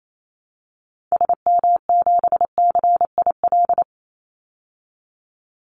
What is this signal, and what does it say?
Morse code, heard as a keyed tone, about 28 wpm, 700 Hz. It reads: SM7CIL
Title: Call Signs Repeated 2x - 28wpm